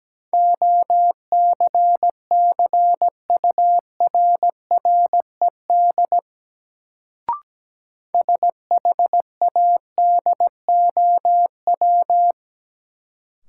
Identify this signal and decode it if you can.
Morse code, heard as a keyed tone, about 17 wpm, 700 Hz. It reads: OCCURRED SHADOW
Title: Sets of 5 Words Encore - New Word Drill - 17wpm